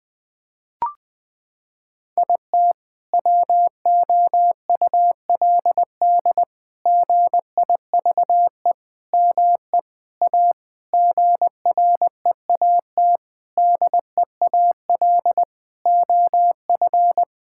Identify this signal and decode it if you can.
Morse code, heard as a keyed tone, about 20 wpm, 700 Hz. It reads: IT WOULD GIVE ME A GREAT DEAL OF
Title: Sentences from Top 1600 Words 20wpm